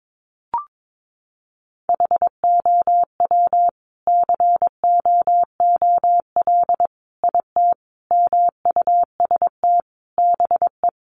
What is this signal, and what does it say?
Morse code, heard as a keyed tone, about 22 wpm, 700 Hz. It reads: HOW COOL IT MUST BE
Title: Sentences from Top 1200 Words 22wpm